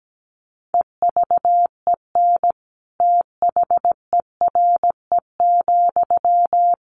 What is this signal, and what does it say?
Morse code, heard as a keyed tone, about 17 wpm, 700 Hz. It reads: EVEN THERE,
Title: Sentences from Top 1500 Words 17wpm